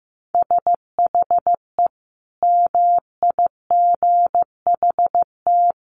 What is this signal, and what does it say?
Morse code, heard as a keyed tone, about 15 wpm, 700 Hz. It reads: SHE MIGHT
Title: Sentences from Top 1000 Words 15wpm